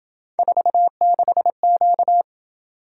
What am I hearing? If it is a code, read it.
Morse code, heard as a keyed tone, about 27 wpm, 700 Hz. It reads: 46Q